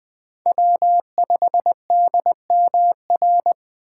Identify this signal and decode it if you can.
Morse code, heard as a keyed tone, about 20 wpm, 700 Hz. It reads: W5DMR
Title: USA Call Signs - 5 Characters - 3x Character Spacing - 20wpm